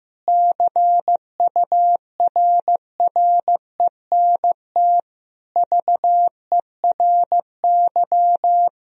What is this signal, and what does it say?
Morse code, heard as a keyed tone, about 16 wpm, 700 Hz. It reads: CURRENT VERY
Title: Sentences from Top 1000 Words - Review Part 4 - 15wpm